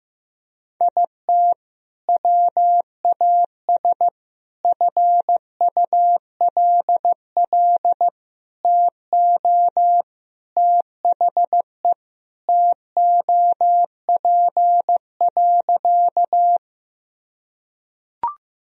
Morse code, heard as a keyed tone, about 15 wpm, 700 Hz. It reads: IT WAS FULL TO THE TOP.
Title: Sentences from Top 1000 Words - Review Part 3 - 15wpm